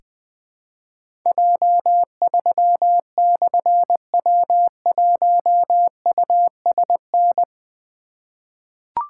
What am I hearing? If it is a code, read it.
Morse code, heard as a keyed tone, about 20 wpm, 700 Hz. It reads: J3/W1USN
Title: Difficult Call Signs 20wpm